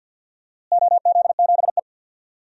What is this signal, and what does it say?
Morse code, heard as a keyed tone, about 47 wpm, 700 Hz. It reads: OZ7E
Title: DXCC-100 Call Signs - 4 Characters - Repeated 3x - 50wpm